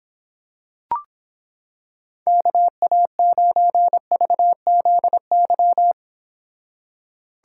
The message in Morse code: KA9VZY

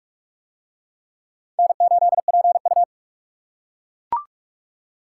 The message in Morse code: N8PU

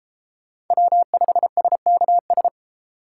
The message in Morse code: W5SKS

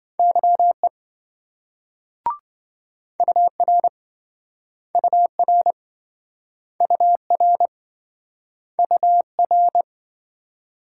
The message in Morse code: YE UR UR UR UR

